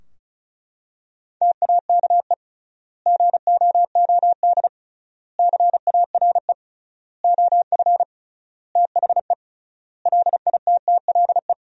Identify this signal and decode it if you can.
Morse code, heard as a keyed tone, about 36 wpm, 700 Hz. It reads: TAKE GOOD CARE OF THE LITTLE
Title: 3x Word-Spacing - Sentences from Top 400 Words - 35wpm